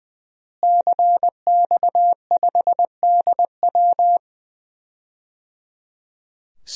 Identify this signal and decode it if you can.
Morse code, heard as a keyed tone, about 21 wpm, 700 Hz. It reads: CX5DW
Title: DXCC-100 Call Signs - 5 Characters - Repeated 3x - 20wpm